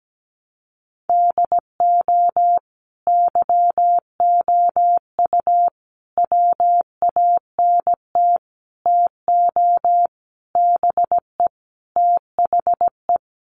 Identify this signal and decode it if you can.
Morse code, heard as a keyed tone, about 17 wpm, 700 Hz. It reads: DO YOU WANT TO BE THE
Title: Sentences from Top 1100 Words 17wpm